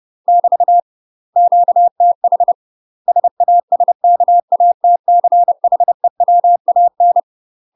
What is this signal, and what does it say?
Morse code, heard as a keyed tone, about 30 wpm, 700 Hz. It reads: <BT> QTH SASKATCHEWAN
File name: UK MASTER QSOs TRAINING / 30WPM / UK MASTER QSOs 0009